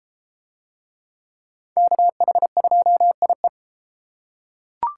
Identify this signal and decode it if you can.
Morse code, heard as a keyed tone, about 31 wpm, 700 Hz. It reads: KH2IE